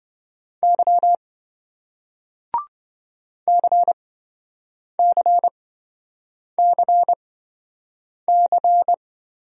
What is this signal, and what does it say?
Morse code, heard as a keyed tone, about 28 wpm, spaced as for 7 wpm, 700 Hz. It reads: Y CCCC